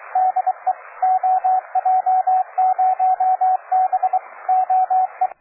Morse code, heard as a keyed tone, about 23 wpm, 750 Hz, with radio noise. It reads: DE OJ0B OE
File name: cw forum / OJ0B sur 1823 le 21-09-2008 17h00 test GP XXL